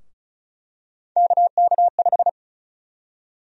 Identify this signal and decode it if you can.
Morse code, heard as a keyed tone, about 36 wpm, 700 Hz. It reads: KK5
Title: USA Call Sign Prefixes - Repeated 2x - 35wpm